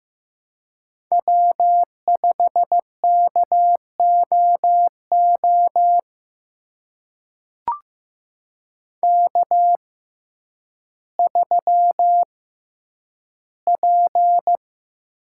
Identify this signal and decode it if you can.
Morse code, heard as a keyed tone, about 15 wpm, spaced as for 9 wpm, 700 Hz. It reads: W5KOO K3P
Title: USA Call Signs - 5 Characters - 6x Character Spacing - 15wpm